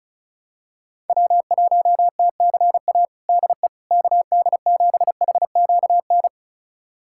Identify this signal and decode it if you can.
Morse code, heard as a keyed tone, about 34 wpm, 700 Hz. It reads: W1TCA DE KD7HQN
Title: USA Callsign DE Callsign 35wpm